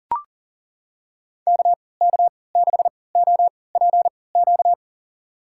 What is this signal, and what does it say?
Morse code, heard as a keyed tone, about 40 wpm, spaced as for 23 wpm, 700 Hz. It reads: KK6OPQ
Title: USA Call Signs - 6 Characters - 3x Character Spacing - 40wpm